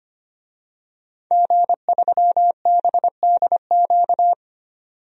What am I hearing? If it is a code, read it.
Morse code, heard as a keyed tone, about 25 wpm, 700 Hz. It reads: G3BDQ
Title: DXCC-100 Call Signs - 5 Characters - 25wpm